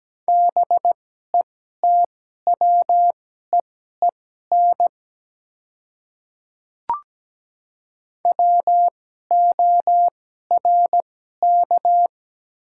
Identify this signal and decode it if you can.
Morse code, heard as a keyed tone, about 17 wpm, spaced as for 12 wpm, 700 Hz. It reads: BETWEEN WORK
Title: 2x Character Spacing - Top 300 Words - 17wpm